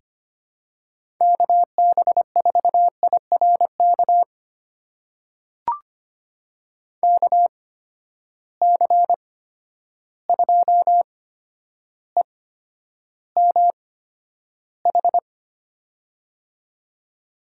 Morse code, heard as a keyed tone, about 25 wpm, spaced as for 11 wpm, 700 Hz. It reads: KB4IRK KC2EMH
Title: USA Call Signs - 6 Characters - 8x Character Spacing - 25wpm